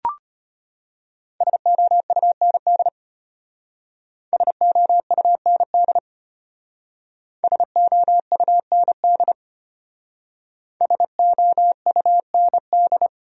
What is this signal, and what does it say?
Morse code, heard as a keyed tone, about 31 wpm, 700 Hz. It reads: SOUND SOUND SOUND SOUND